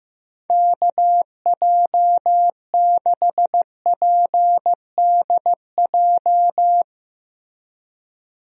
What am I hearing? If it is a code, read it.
Morse code, heard as a keyed tone, about 15 wpm, 700 Hz. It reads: KJ6PDJ
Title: USA Call Signs - 6 Characters - Repeated 2x - 15wpm